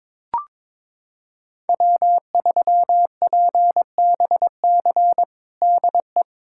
Morse code, heard as a keyed tone, about 22 wpm, 700 Hz. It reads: W3PBC DE
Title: USA Callsign DE Callsign 22wpm